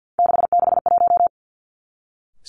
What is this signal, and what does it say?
Morse code, heard as a keyed tone, about 50 wpm, 700 Hz. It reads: 661
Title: Three Challenging Letter-Number Combinations 50wpm